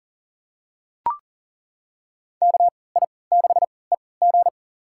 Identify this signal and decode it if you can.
Morse code, heard as a keyed tone, about 39 wpm, spaced as for 22 wpm, 700 Hz. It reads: KI6EG
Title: USA Call Signs - 5 Characters - 3x Character Spacing - 40wpm